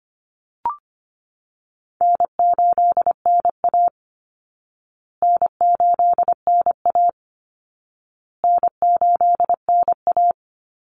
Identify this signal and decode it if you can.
Morse code, heard as a keyed tone, about 25 wpm, 700 Hz. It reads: N8NA N8NA N8NA
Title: Call Signs Repeated 2x - 25wpm